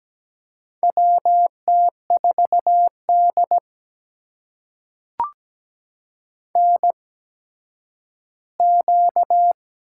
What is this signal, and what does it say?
Morse code, heard as a keyed tone, about 17 wpm, spaced as for 10 wpm, 700 Hz. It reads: WT4D NQ
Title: USA Call Signs - 4 Characters - 8x Character Spacing - 17wpm